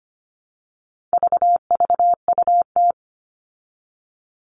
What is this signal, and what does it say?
Morse code, heard as a keyed tone, about 25 wpm, 700 Hz. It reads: VVUT
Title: All Cut Numbers - 4-Digit 25wpm